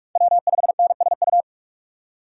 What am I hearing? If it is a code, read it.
Morse code, heard as a keyed tone, about 41 wpm, 700 Hz. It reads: W5NSU